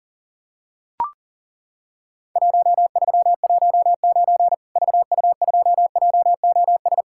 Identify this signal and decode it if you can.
Morse code, heard as a keyed tone, about 40 wpm, 700 Hz. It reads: 1319 VU2JOS
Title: Worked All Europe DX Contest - QTC Exchange - 40wpm